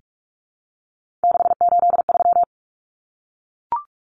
Morse code, heard as a keyed tone, about 44 wpm, 700 Hz. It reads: 683
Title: Numbers - 3-Digit 45wpm